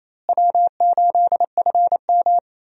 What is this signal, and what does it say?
Morse code, heard as a keyed tone, about 28 wpm, 700 Hz. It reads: W8FM
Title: USA Call Signs - 4 Characters - Repeated 2x - 28wpm